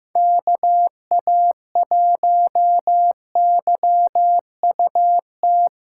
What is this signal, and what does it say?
Morse code, heard as a keyed tone, about 15 wpm, 700 Hz. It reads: KA1YUT
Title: USA Callsign DE Callsign 15wpm